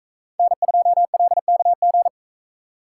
Morse code, heard as a keyed tone, about 41 wpm, 700 Hz. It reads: N2LKG